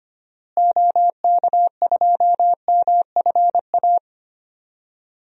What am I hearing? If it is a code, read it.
Morse code, heard as a keyed tone, about 25 wpm, 700 Hz. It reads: OK2MFA
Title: DXCC-100 Call Signs - 6 Characters - Repeated 3x - 25wpm